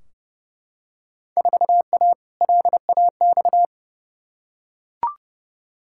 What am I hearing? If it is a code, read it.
Morse code, heard as a keyed tone, about 30 wpm, 700 Hz. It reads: 4A LAX
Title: Field Day Exchanges - 30wpm